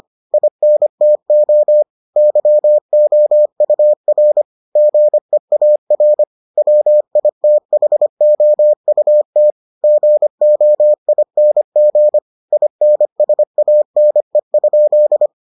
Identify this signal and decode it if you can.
Morse code, heard as a keyed tone, about 25 wpm, 600 Hz. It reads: INTO YOUR GEAR WITHOUT GOING INSANE?